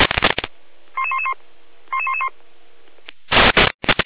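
Five-tone selective call (ZVEI), heard as quick runs of five tones, tones of about 70 ms. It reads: 11111 11111
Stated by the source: Five-tone ZVEI selective call received on an SDR.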